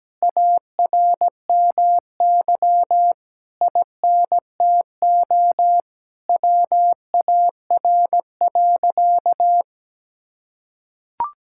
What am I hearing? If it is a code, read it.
Morse code, heard as a keyed tone, about 17 wpm, 700 Hz. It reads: ARMY INTO WAR.
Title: Sentences from Top 1000 Words - Review Part 5 - 17wpm